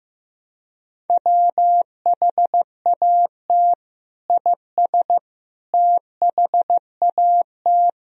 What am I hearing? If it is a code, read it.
Morse code, heard as a keyed tone, about 16 wpm, 700 Hz. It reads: WHAT IS THAT
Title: Sentences from Top 1000 Words - Review Part 1 - 15wpm